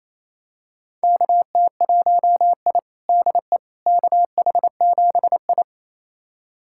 Morse code, heard as a keyed tone, about 28 wpm, 700 Hz. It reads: KT1I DE KH7I
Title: USA Callsign DE Callsign - 4 Characters - 28wpm